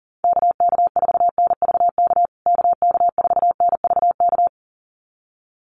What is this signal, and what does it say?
Morse code, heard as a keyed tone, about 40 wpm, 700 Hz. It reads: KK4NVK KK4NVK
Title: USA Call Signs Repeated 2x 40wpm